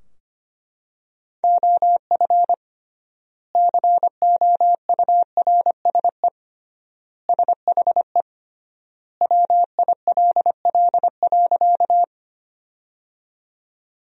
Morse code, heard as a keyed tone, about 25 wpm, 700 Hz. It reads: OF COURSE SHE WILL.
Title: 3x Word-Spacing - Sentences from Top 500 Words - 25wpm